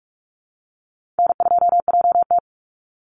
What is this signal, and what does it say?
Morse code, heard as a keyed tone, about 44 wpm, 700 Hz. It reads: N2JT
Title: Easy Call Signs - 45wpm